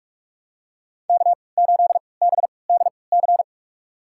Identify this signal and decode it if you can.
Morse code, heard as a keyed tone, about 42 wpm, spaced as for 25 wpm, 700 Hz. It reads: K8BDC
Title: USA Call Signs - 5 Characters - 3x Character Spacing - 45wpm